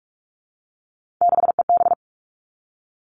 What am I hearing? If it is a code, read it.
Morse code, heard as a keyed tone, about 46 wpm, 700 Hz. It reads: 6EB
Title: Three Challenging Letter-Number Combinations 45wpm